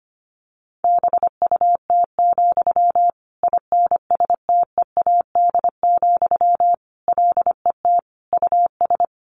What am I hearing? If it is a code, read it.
Morse code, heard as a keyed tone, about 25 wpm, 700 Hz. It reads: BUT, INSTEAD, LET US
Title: Sentences from Top 1500 Words 25wpm